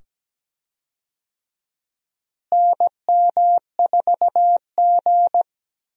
Morse code, heard as a keyed tone, about 17 wpm, 700 Hz. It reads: NM4G
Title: USA Call Signs - 4 Characters - Repeated 2x - 17wpm